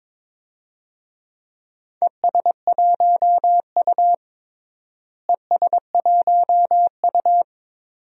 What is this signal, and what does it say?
Morse code, heard as a keyed tone, about 21 wpm, 700 Hz. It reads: ES1U ES1U
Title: DXCC-100 Call Signs - 4 Characters - Repeated 2x - 22wpm